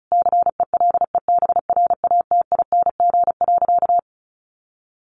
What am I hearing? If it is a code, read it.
Morse code, heard as a keyed tone, about 34 wpm, 700 Hz. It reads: CELEBRATING.
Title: A Sound Of Thunder 35wpm